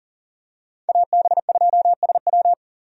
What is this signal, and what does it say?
Morse code, heard as a keyed tone, about 40 wpm, 700 Hz. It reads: AB2SW
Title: USA Call Signs - 5 Characters - 40wpm